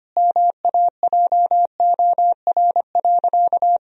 Morse code, heard as a keyed tone, about 25 wpm, 700 Hz. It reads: MAJOR.
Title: Sentences from Top 1000 Words - Review Part 4 - 25wpm